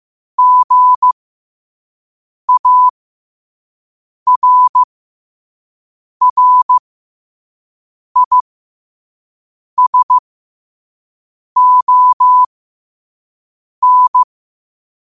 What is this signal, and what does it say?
Morse code, heard as a keyed tone, about 15 wpm, spaced as for 5 wpm, 1000 Hz. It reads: GARRISON
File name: qotdfiles / 5wpm